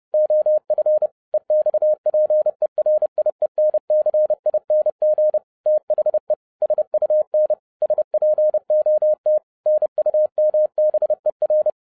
Morse code, heard as a keyed tone, about 30 wpm, 610 Hz. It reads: OF EXPERIENCING THE SUN SPOT NUMBER
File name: 2012-01-mp3 / n11-I DIDNT MISS THE SOLAR CYCLE 24-30wpm